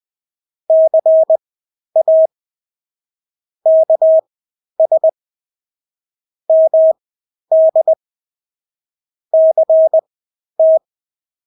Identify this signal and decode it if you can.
Morse code, heard as a keyed tone, about 20 wpm, spaced as for 11 wpm, 640 Hz. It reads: CA KS MD CT
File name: State-Abbreviations-10